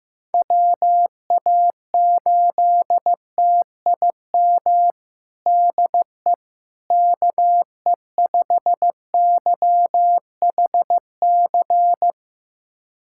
Morse code, heard as a keyed tone, about 15 wpm, 700 Hz. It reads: WA8TIM DE KE5YHC
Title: USA Callsign DE Callsign 15wpm